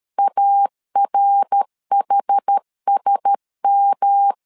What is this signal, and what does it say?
Morse code, heard as a keyed tone, about 13 wpm, 800 Hz. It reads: ARHSM
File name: LEZIONI / 4 Q1S 12wpm 1802